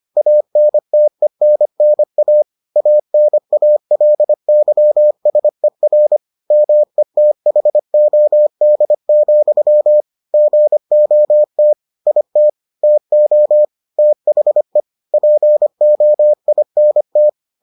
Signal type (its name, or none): Morse code